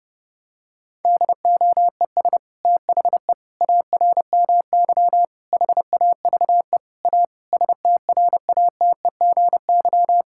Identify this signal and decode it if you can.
Morse code, heard as a keyed tone, about 30 wpm, 700 Hz. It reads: DOES THE ARMY HAVE A STRATEGY
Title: Sentences from Top 1000 Words - Review Part 5 - 30wpm